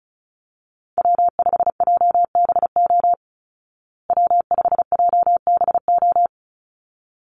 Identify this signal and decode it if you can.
Morse code, heard as a keyed tone, about 34 wpm, 700 Hz. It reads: W5JBO W5JBO
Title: Call Signs Repeated 3x - 35wpm